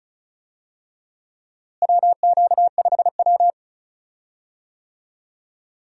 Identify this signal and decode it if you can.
Morse code, heard as a keyed tone, about 36 wpm, 700 Hz. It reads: WQ5W
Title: USA Call Signs - 4 Characters - Repeated 2x - 35wpm